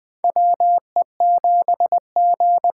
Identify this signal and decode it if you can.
Morse code, heard as a keyed tone, about 21 wpm, 700 Hz. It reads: WE7G
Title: K1USN SST QSO Practice - 20wpm